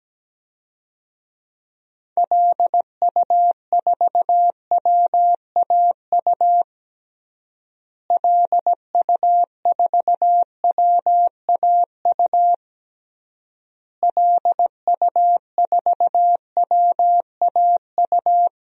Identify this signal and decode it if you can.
Morse code, heard as a keyed tone, about 17 wpm, 700 Hz. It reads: LU4WAU LU4WAU LU4WAU
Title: DXCC-100 Call Signs - 6 Characters - Repeated 3x - 17wpm